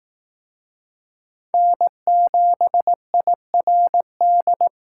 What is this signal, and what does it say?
Morse code, heard as a keyed tone, about 18 wpm, 700 Hz. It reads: N7IRD
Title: Speed Racing - USA 1x3 Call Signs 15wpm